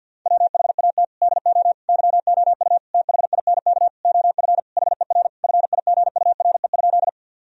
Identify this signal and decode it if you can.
Morse code, heard as a keyed tone, about 52 wpm, 700 Hz. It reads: WHAT DO YOU THINK OF HER FIGURE?
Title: Sentences from Top 1000 Words - Review Part 2 - 50wpm